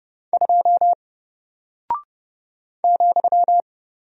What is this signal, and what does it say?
Morse code, heard as a keyed tone, about 30 wpm, 700 Hz. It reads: 2 ,
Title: Single Character (Letter:Number:Punctuation:Prosign) Alt 30wpm